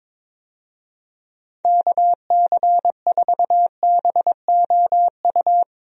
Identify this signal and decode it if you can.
Morse code, heard as a keyed tone, about 22 wpm, 700 Hz. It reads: KC4BOU
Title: USA Call Signs - 6 Characters - Repeated 2x - 22wpm